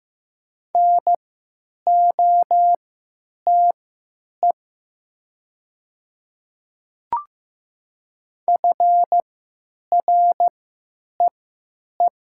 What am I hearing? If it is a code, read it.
Morse code, heard as a keyed tone, about 15 wpm, spaced as for 9 wpm, 700 Hz. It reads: NOTE FREE
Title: 3x Character Spacing - Top 500 Words - 15wpm